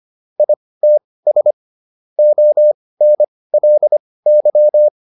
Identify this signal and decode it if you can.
Morse code, heard as a keyed tone, about 25 wpm, spaced as for 18 wpm, 600 Hz. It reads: ITS ONLY